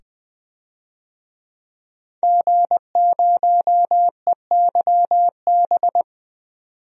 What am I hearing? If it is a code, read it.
Morse code, heard as a keyed tone, about 20 wpm, 700 Hz. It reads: G0EYB
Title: DXCC-100 Call Signs - 5 Characters - 20wpm